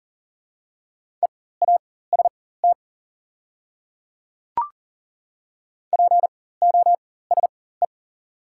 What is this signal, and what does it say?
Morse code, heard as a keyed tone, about 37 wpm, spaced as for 18 wpm, 690 Hz. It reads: EAST POSE
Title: 4x Character Spacing - Top 500 Words - 40wpm